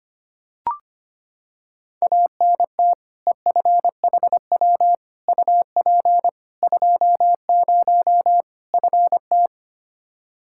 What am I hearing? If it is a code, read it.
Morse code, heard as a keyed tone, about 25 wpm, 700 Hz. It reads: ANT EFHW UP 20 FT
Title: Practice QSO - Exchange 2 - Info - 25wpm